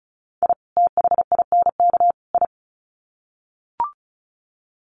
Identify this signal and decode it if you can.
Morse code, heard as a keyed tone, about 34 wpm, 700 Hz. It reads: I THINK I
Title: Sets of 3 Words 35wpm